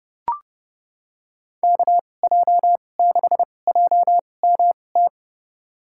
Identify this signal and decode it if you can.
Morse code, heard as a keyed tone, about 31 wpm, spaced as for 22 wpm, 700 Hz. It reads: KJ6JMT
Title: USA Call Signs - 6 Characters - 2x Character Spacing - 30wpm